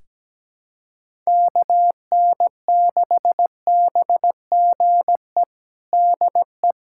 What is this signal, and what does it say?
Morse code, heard as a keyed tone, about 17 wpm, 700 Hz. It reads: KN6BGE DE
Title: USA Callsign DE Callsign 17wpm